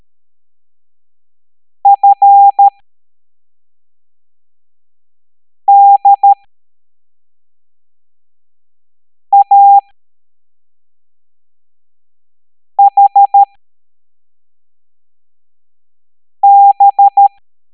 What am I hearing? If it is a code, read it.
Morse code, heard as a keyed tone, about 13 wpm, spaced as for 3 wpm, 800 Hz. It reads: FDAHB
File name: hamdata / L07-EX03-HQP